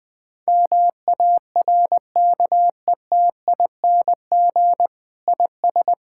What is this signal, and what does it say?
Morse code, heard as a keyed tone, about 21 wpm, 700 Hz. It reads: MARKETING IS
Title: Sentences from Top 1000 Words - Review Part 5 - 20wpm